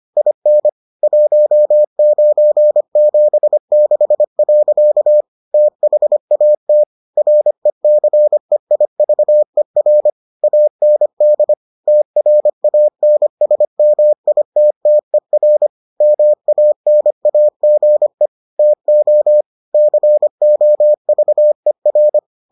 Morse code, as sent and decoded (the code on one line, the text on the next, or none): .. -. / .---- ----. --... -.... .-.-.- / - .... .- - / .-. -.-. .. ...- . .-. / .- -. -.. / - .-. .- -. ... -- .. - - . .-. / -- .- -. .- --. / - --- / -.-. --- ...- . .-.
IN 1976. THAT RECEIVER AND TRANSMITTER MANAGE TO COVER